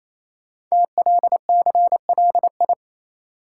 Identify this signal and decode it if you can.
Morse code, heard as a keyed tone, about 27 wpm, 700 Hz. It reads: TLCLI
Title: Code Groups - 5 Letter:Numbers - 28wpm